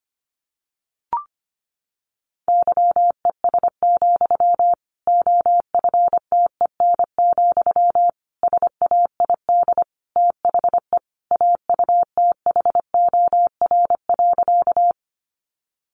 Morse code, heard as a keyed tone, about 25 wpm, 700 Hz. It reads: YES, OFTEN, SAID THE AUTHOR.
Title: Sentences from Top 1200 Words 25wpm